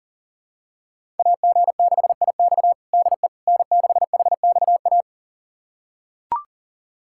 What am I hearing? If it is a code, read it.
Morse code, heard as a keyed tone, about 40 wpm, 700 Hz. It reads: AG6IX DE N6HXA
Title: USA Callsign DE Callsign - 5 Characters - 40wpm